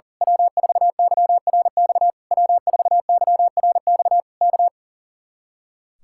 Morse code, heard as a keyed tone, about 37 wpm, 700 Hz. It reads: W4YRX W4YRX K
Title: Speed-Racing Common QSO Phrases - 40wpm